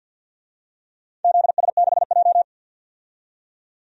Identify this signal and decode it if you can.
Morse code, heard as a keyed tone, about 46 wpm, 700 Hz. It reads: ZS6J
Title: DXCC-100 Call Signs - 4 Characters - 50wpm